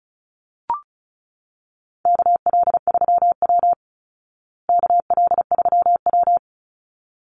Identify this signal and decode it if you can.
Morse code, heard as a keyed tone, about 34 wpm, 700 Hz. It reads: KL3W KL3W
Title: Call Signs Repeated 3x - 35wpm